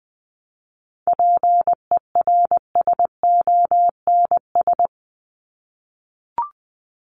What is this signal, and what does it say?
Morse code, heard as a keyed tone, about 20 wpm, 700 Hz. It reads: PERSONS
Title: ES:S Suffix - Top 100 Words with Base Words 20wpm